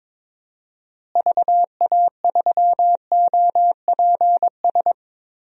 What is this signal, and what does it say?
Morse code, heard as a keyed tone, about 22 wpm, 700 Hz. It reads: VA3OPS
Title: DXCC-100 Call Signs - 6 Characters - Repeated 2x - 22wpm